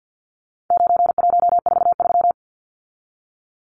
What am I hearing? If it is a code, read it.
Morse code, heard as a keyed tone, about 50 wpm, 700 Hz. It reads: 9143
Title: Numbers - 4-Digit 50wpm